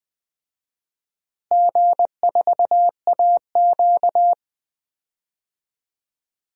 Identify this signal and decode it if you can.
Morse code, heard as a keyed tone, about 20 wpm, 700 Hz. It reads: G4AQ
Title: DXCC-100 Call Signs - 4 Characters - Repeated 2x - 20wpm